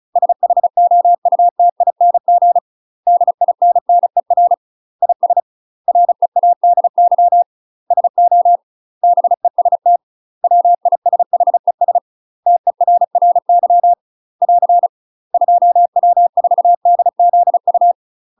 Morse code, heard as a keyed tone, about 34 wpm, 700 Hz. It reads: SHOUTING DINNER IS READY SO BEST WISHES TERRY <AR> 2W4DZU
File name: UK MASTER QSOs TRAINING / 35WPM / UK MASTER QSOs 0025